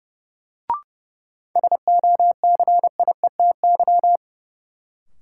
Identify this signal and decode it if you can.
Morse code, heard as a keyed tone, about 30 wpm, 700 Hz. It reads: SOCIETY
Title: Sentences from Top 700 Words - New Word Drill - 30wpm